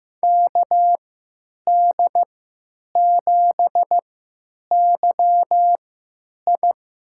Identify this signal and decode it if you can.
Morse code, heard as a keyed tone, about 15 wpm, spaced as for 9 wpm, 700 Hz. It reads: KD7YI
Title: USA Call Signs - 3x Character Spacing - 15wpm